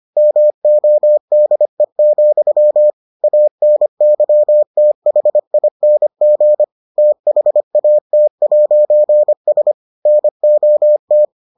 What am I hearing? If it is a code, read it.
Morse code, heard as a keyed tone, about 25 wpm, 600 Hz. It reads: MODE, ANYTHING THAT'S NOT